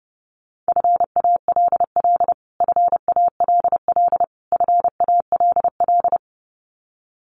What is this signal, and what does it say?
Morse code, heard as a keyed tone, about 28 wpm, 700 Hz. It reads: FALL FALL FALL
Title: Top 500 Words Repeated 30wpm